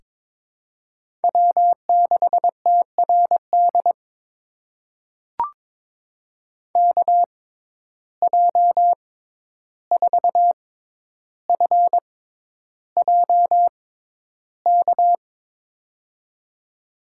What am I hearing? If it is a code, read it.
Morse code, heard as a keyed tone, about 22 wpm, spaced as for 11 wpm, 700 Hz. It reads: W6TRD KJ4FJK
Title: USA Call Signs - 6x Character Spacing - 22wpm